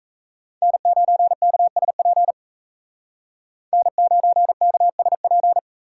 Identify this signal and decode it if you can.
Morse code, heard as a keyed tone, about 37 wpm, 700 Hz. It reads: N9KSP N9KSP